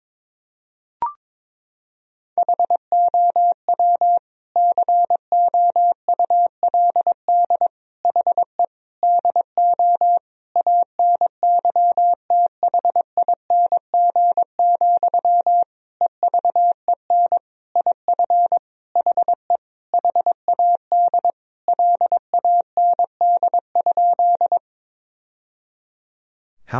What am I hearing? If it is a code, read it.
Morse code, heard as a keyed tone, about 22 wpm, 700 Hz. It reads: HOW COULD HE DO ANYTHING, EVEN IF HE HAD LAND?
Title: Sentences from Top 1000 Words - Review Part 1 - 22wpm